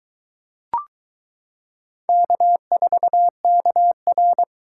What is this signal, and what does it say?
Morse code, heard as a keyed tone, about 22 wpm, 700 Hz. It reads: K4KR